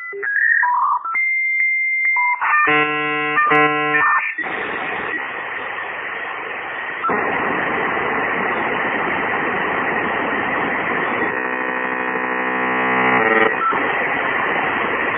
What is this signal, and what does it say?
Dial-up modem handshake.